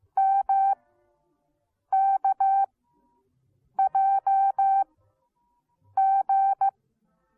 Morse code, heard as a keyed tone, about 15 wpm, spaced as for 6 wpm, 780 Hz. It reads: MKJG